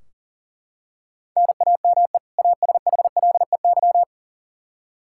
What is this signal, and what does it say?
Morse code, heard as a keyed tone, about 40 wpm, 700 Hz. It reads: NAME ASHLEY
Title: Practice QSO - Exchange 1 - Beginning - 40wpm